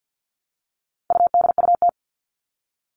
Morse code, heard as a keyed tone, about 50 wpm, 700 Hz. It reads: UDUT
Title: All Cut Numbers - 4-Digit 50wpm